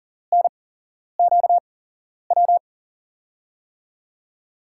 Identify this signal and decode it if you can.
Morse code, heard as a keyed tone, about 42 wpm, spaced as for 11 wpm, 700 Hz. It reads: NQW
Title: USA Call Signs - 6 Characters - 8x Character Spacing - 40wpm